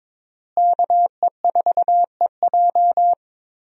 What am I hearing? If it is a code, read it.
Morse code, heard as a keyed tone, about 22 wpm, 700 Hz. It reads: KE4EJ
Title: USA Call Signs - 5 Characters - Repeated 3x - 22wpm